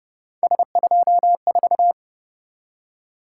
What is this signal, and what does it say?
Morse code, heard as a keyed tone, about 31 wpm, 700 Hz. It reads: S24
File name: Speed-Racing - Three Letter-Number Combinations - 25wpm